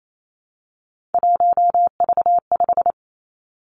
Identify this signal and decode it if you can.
Morse code, heard as a keyed tone, about 28 wpm, 700 Hz. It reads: 1V5
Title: Three Challenging Letter-Number Combinations 28wpm